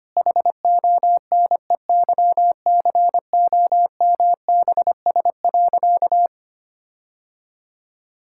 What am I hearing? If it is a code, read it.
Morse code, heard as a keyed tone, about 25 wpm, 700 Hz. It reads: HONEYCOMBS.
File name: practice / 25wpm / All-Age Appropriate Jokes - 25wpm